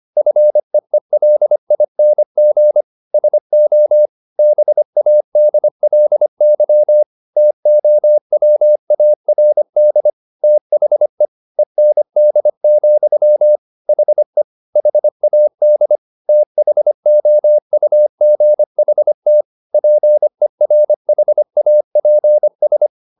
Morse code, heard as a keyed tone, about 25 wpm, 600 Hz. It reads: FEELING SO BADLY TOWARD THE END, HE HAD THOUGHT PERHAPS